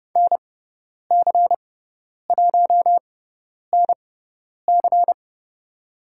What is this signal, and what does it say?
Morse code, heard as a keyed tone, about 30 wpm, spaced as for 10 wpm, 700 Hz. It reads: NC1NC
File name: Lesson 16g - Letter c - Call Signs 30wpm